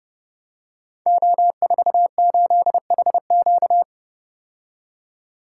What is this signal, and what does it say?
Morse code, heard as a keyed tone, about 30 wpm, 700 Hz. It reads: O48HQ
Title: Code Groups - 5 Letter:Numbers - 30wpm